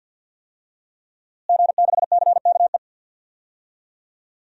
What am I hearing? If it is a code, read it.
Morse code, heard as a keyed tone, about 51 wpm, 700 Hz. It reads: G6CKE
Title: DXCC-100 Call Signs - 5 Characters - Repeated 2x - 50wpm